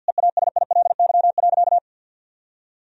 Morse code, heard as a keyed tone, about 48 wpm, 700 Hz. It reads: EASILY.
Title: Sentences from Top 1000 Words - Review Part 1 - 50wpm